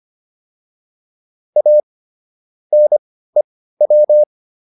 Morse code, heard as a keyed tone, about 25 wpm, spaced as for 15 wpm, 600 Hz. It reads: A NEW